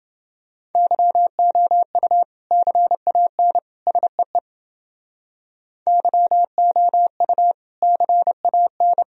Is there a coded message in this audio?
Yes: Morse code from 0.8 to 9.0 s, heard as a keyed tone, about 29 wpm, 700 Hz. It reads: YOU CAN SEE YOU CAN